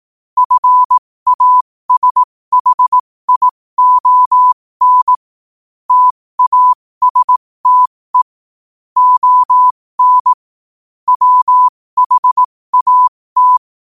Morse code, heard as a keyed tone, about 18 wpm, spaced as for 15 wpm, 1000 Hz. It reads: FASHION TASTE ON WHAT
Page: New quotes every day in morse code at 15 Words per minute.